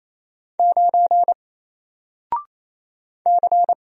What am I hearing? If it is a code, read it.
Morse code, heard as a keyed tone, about 28 wpm, 700 Hz. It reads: 9 C
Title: Single Character (Letter:Number:Punctuation:Prosign) Alt 28wpm